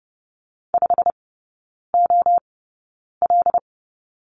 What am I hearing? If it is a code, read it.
Morse code, heard as a keyed tone, about 30 wpm, spaced as for 9 wpm, 700 Hz. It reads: 5OL
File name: course / 30:10wpm / 087 Lesson 15e - Number 5 - Three Characters 30wpm